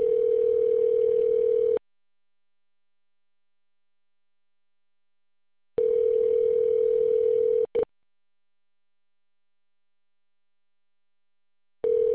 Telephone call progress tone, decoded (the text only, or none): ringback tone